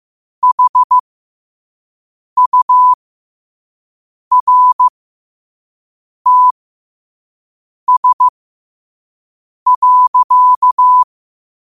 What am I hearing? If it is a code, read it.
Morse code, heard as a keyed tone, about 15 wpm, spaced as for 5 wpm, 1000 Hz. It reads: HURTS.
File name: qotdfiles / 5wpm